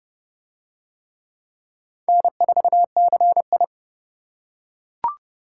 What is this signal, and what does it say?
Morse code, heard as a keyed tone, about 28 wpm, 700 Hz. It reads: N4CI